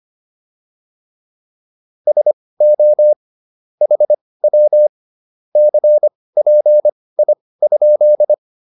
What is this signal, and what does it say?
Morse code, heard as a keyed tone, about 25 wpm, spaced as for 18 wpm, 600 Hz. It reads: SO HW CPI?